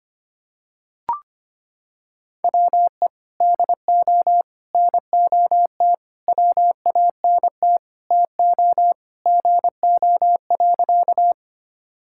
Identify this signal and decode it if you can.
Morse code, heard as a keyed tone, about 25 wpm, 700 Hz. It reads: WE DO NOT WANT TO GO.
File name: practice / 25wpm / Sentences with 4-Word Phrases - 25wpm